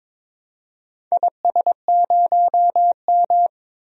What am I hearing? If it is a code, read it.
Morse code, heard as a keyed tone, about 22 wpm, 700 Hz. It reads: IS0M
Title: DXCC-100 Call Signs - 4 Characters - 22wpm